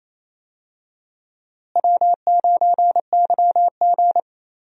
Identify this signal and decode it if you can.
Morse code, heard as a keyed tone, about 28 wpm, 700 Hz. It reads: W9YG
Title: USA Call Signs - 4 Characters - Repeated 2x - 28wpm